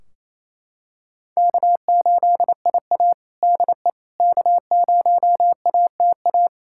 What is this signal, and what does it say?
Morse code, heard as a keyed tone, about 27 wpm, 700 Hz. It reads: K8IA DE K0ATA
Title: Practice QSO - Exchange 4 - Ending - 28wpm